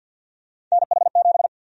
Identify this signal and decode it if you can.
Morse code, heard as a keyed tone, about 52 wpm, 700 Hz. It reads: NH7
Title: USA Call Sign Prefixes - 50wpm